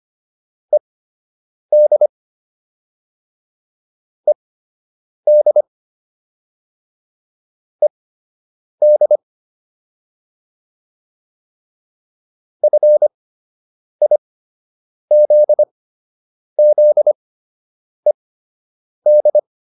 Morse code, heard as a keyed tone, about 25 wpm, spaced as for 8 wpm, 600 Hz. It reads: ED ED ED FIZZED
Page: fundamental-practice-files